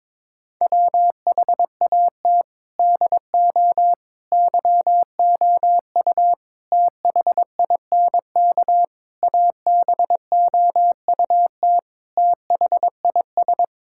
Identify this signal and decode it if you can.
Morse code, heard as a keyed tone, about 22 wpm, 700 Hz. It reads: WHAT DO YOU THINK ABOUT THIS
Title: Sentences from Top 1000 Words - Review Part 5 - 22wpm